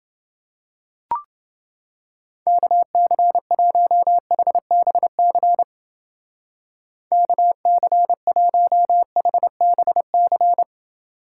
Morse code, heard as a keyed tone, about 30 wpm, 700 Hz. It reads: KC1HBC KC1HBC